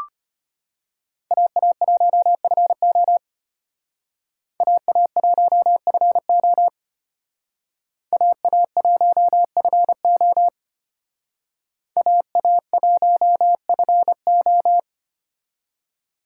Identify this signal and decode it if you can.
Morse code, heard as a keyed tone, about 32 wpm, 700 Hz. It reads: AA1FO AA1FO AA1FO AA1FO